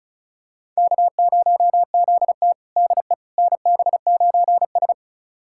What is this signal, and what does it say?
Morse code, heard as a keyed tone, about 36 wpm, 700 Hz. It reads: K0ZT DE NB9S
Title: USA Callsign DE Callsign - 4 Characters - 35wpm